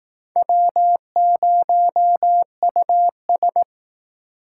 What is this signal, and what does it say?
Morse code, heard as a keyed tone, about 18 wpm, 700 Hz. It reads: W0US
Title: Speed Racing - USA 1x2 Call Signs 15wpm